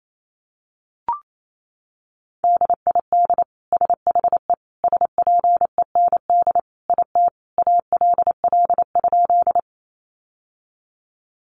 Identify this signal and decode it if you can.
Morse code, heard as a keyed tone, about 28 wpm, 700 Hz. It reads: DID SHE SPEND IT ALL?
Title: Sentences from Top 800 Words 28wpm